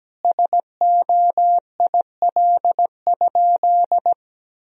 Morse code, heard as a keyed tone, about 17 wpm, 700 Hz. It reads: SOIL?
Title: Sentences from Top 1000 Words - Review Part 4 - 17wpm